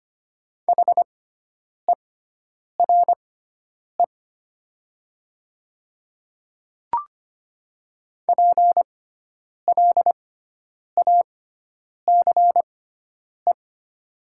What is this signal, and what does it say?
Morse code, heard as a keyed tone, about 25 wpm, spaced as for 9 wpm, 700 Hz. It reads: HERE PLACE
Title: 6x Character Spacing - Top 300 Words - 25wpm